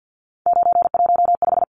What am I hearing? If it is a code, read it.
Morse code, heard as a keyed tone, about 50 wpm, 700 Hz. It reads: 915
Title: Numbers - 3-Digit 50wpm